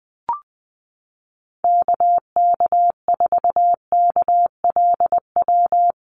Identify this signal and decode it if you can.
Morse code, heard as a keyed tone, about 20 wpm, 700 Hz. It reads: KK4KLW
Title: USA Call Signs Repeated 3x 20wpm